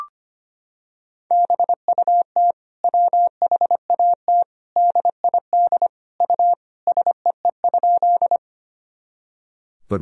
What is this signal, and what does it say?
Morse code, heard as a keyed tone, about 25 wpm, 700 Hz. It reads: BUT WHAT DID U SEE?
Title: Sentences from Top 100 Words and Abbreviations 25wpm